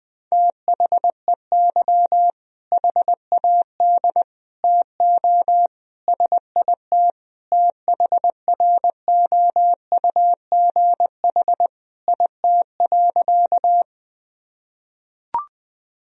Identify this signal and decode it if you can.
Morse code, heard as a keyed tone, about 21 wpm, 700 Hz. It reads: THEY HAD TO SIT THROUGH IT.
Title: Sentences from Top 1000 Words - Review Part 1 - 20wpm